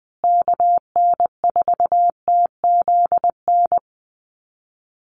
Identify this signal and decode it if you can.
Morse code, heard as a keyed tone, about 20 wpm, 700 Hz. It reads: KN4TZN
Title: USA Call Signs Repeated 2x 20wpm